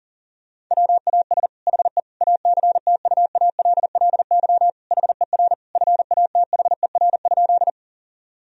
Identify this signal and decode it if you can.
Morse code, heard as a keyed tone, about 40 wpm, 690 Hz. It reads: WAS HE ACTUALLY HER FATHER?
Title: Sentences from Top 1000 Words - Review Part 4 - 40wpm